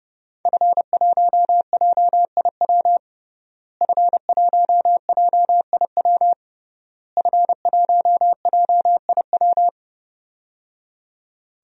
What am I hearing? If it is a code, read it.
Morse code, heard as a keyed tone, about 30 wpm, 700 Hz. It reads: F1JIW F1JIW F1JIW
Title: DXCC-100 Call Signs - 5 Characters - Repeated 3x - 30wpm